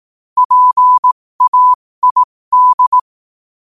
Morse code, heard as a keyed tone, about 18 wpm, spaced as for 15 wpm, 1000 Hz. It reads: PAID
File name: qotdfiles / 15wpm